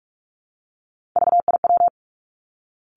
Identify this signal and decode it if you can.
Morse code, heard as a keyed tone, about 46 wpm, 700 Hz. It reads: VIW
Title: Three Challenging Letter-Number Combinations 45wpm